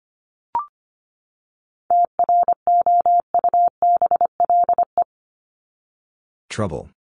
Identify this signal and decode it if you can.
Morse code, heard as a keyed tone, about 25 wpm, 700 Hz. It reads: TROUBLE
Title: Top 1001-1250 Words 25wpm